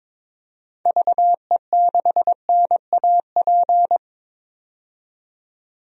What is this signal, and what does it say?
Morse code, heard as a keyed tone, about 22 wpm, 700 Hz. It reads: VE6NAP
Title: DXCC-100 Call Signs - 6 Characters - 22wpm